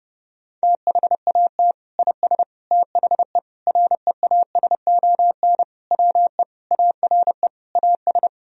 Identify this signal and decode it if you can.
Morse code, heard as a keyed tone, about 30 wpm, 700 Hz. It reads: THAT IS THE REASON WE ARE AS
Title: Sentences from Top 1000 Words - Review Part 4 - 30wpm